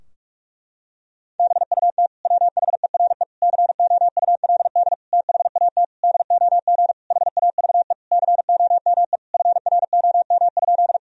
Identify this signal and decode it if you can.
Morse code, heard as a keyed tone, about 46 wpm, 700 Hz. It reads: BUT WHERE COULD THAT DOG HAVE COME FROM?
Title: Sentences from Top 1000 Words - Review Part 2 - 45wpm